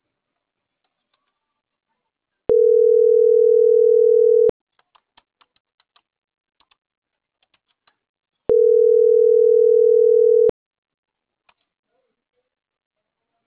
Ringback tone (North America), 440 + 480 Hz.